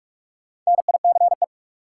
Morse code, heard as a keyed tone, about 46 wpm, 700 Hz. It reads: NICE
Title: Sentences from Top 1100 Words - New Word Drill - 45wpm